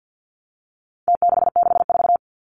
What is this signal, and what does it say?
Morse code, heard as a keyed tone, about 50 wpm, 700 Hz. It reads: T664
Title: Numbers - Contest Style - 4-Digit 50wpm